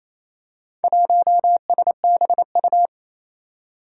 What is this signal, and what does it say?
Morse code, heard as a keyed tone, about 28 wpm, 700 Hz. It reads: 1SBU
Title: Four Challenging Letter-Number Combinations 28wpm